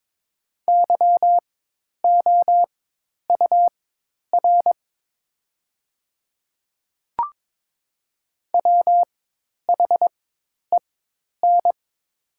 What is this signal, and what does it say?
Morse code, heard as a keyed tone, about 22 wpm, spaced as for 10 wpm, 700 Hz. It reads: YOUR WHEN
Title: 4x Character Spacing - Top 200 Words - 22wpm